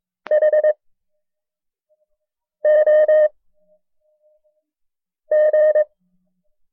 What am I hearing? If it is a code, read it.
Morse code, heard as a keyed tone, about 22 wpm, 610 Hz. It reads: H O G
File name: FreeFiles / CW-practice / KB6MT-tape1-side2